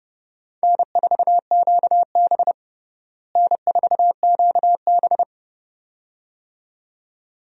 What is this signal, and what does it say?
Morse code, heard as a keyed tone, about 30 wpm, 700 Hz. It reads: N4QB N4QB
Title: USA Call Signs - 4 Characters - Repeated 2x - 30wpm